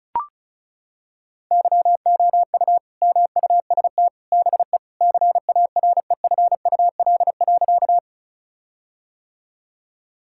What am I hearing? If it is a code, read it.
Morse code, heard as a keyed tone, about 34 wpm, 700 Hz. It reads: YOU MUST BE CAREFUL.
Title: Sentences from Top 1600 Words 35wpm